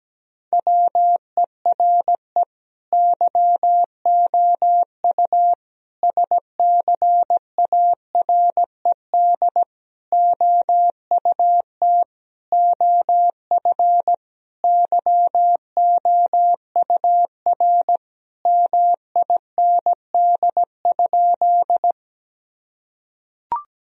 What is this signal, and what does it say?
Morse code, heard as a keyed tone, about 17 wpm, 700 Hz. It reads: WERE YOU SCARED OUT OF YOUR MIND?
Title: Sentences from Top 1000 Words - Review Part 4 - 17wpm